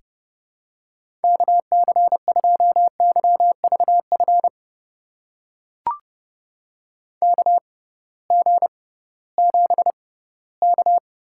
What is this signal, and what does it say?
Morse code, heard as a keyed tone, about 30 wpm, spaced as for 18 wpm, 700 Hz. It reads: KC2YVF KG7K
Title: USA Call Signs - 6 Characters - 6x Character Spacing - 30wpm